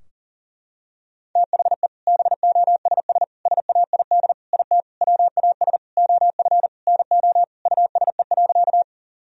Morse code, heard as a keyed tone, about 40 wpm, 700 Hz. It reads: THE BOSS SAID IT WAS OF NO USE.
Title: Sentences from Top 1000 Words - Review Part 4 - 40wpm